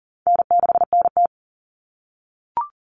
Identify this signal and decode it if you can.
Morse code, heard as a keyed tone, about 40 wpm, 700 Hz. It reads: N6NT
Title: Common Cut Numbers - 4-Digit 40wpm